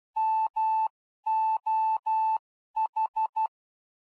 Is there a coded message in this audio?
Yes: Morse code from 0.2 to 3.5 s, heard as a keyed tone, about 12 wpm, 880 Hz. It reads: MOH